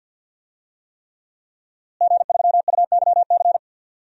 Morse code, heard as a keyed tone, about 52 wpm, 700 Hz. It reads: G3UYC
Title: DXCC-100 Call Signs - 5 Characters - Repeated 3x - 50wpm